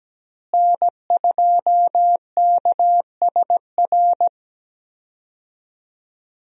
Morse code, heard as a keyed tone, about 17 wpm, 700 Hz. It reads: N2KSR
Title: USA Call Signs - 5 Characters - Repeated 3x - 17wpm